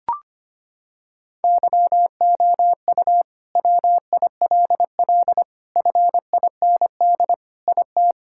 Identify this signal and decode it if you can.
Morse code, heard as a keyed tone, about 25 wpm, 700 Hz. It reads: YOU WILL FIND IT
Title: Sentences from Top 1000 Words - Review Part 5 - 25wpm